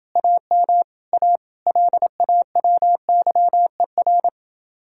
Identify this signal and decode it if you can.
Morse code, heard as a keyed tone, about 29 wpm, 700 Hz. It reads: AM A LAWYER